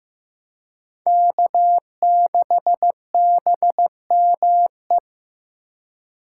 Morse code, heard as a keyed tone, about 15 wpm, 700 Hz. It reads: K6BME
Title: USA Call Signs - 5 Characters - 4x Character Spacing - 15wpm